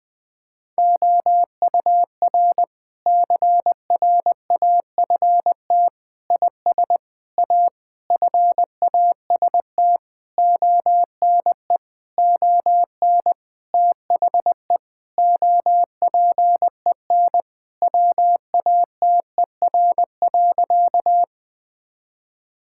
Morse code, heard as a keyed tone, about 20 wpm, 700 Hz. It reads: OUR CRAFT IS A FAST ONE ON THE OPEN WATER.
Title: Sentences from Top 1000 Words - Review Part 6 - 20wpm